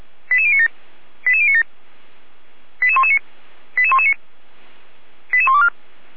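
Five-tone selective call (ZVEI), heard as quick runs of five tones, tones of about 80 ms. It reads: 88097 88097 88119 88119 88124
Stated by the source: ZVEI five-tone call received by SDR.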